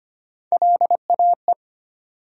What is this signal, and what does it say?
Morse code, heard as a keyed tone, about 25 wpm, 700 Hz. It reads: LAE
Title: USA Call Sign Postfixes - 3 Letters 25wpm